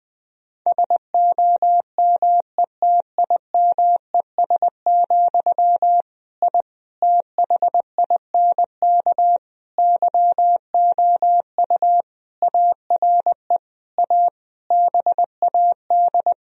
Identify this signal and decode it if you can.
Morse code, heard as a keyed tone, about 20 wpm, 700 Hz. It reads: SOMETIMES, I THINK YOU ARE A BAD
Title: Sentences from Top 1000 Words - Review Part 3 - 20wpm